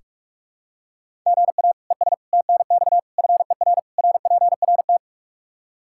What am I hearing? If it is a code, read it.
Morse code, heard as a keyed tone, about 46 wpm, 700 Hz. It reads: GA ES TNX FER RPRT
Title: K1USN SST QSO Practice 45wpm